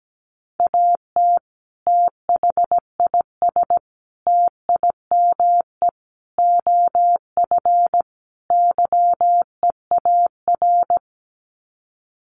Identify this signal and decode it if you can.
Morse code, heard as a keyed tone, about 17 wpm, 700 Hz. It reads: AT THIS TIME OF YEAR
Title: Sets of 5 Words Encore2 17wpm